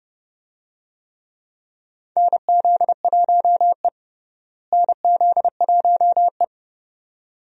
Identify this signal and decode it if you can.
Morse code, heard as a keyed tone, about 30 wpm, 700 Hz. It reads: NZ1E NZ1E
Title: USA Call Signs - 4 Characters - Repeated 2x - 30wpm